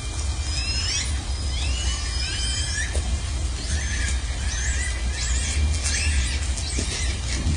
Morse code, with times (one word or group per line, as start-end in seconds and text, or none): none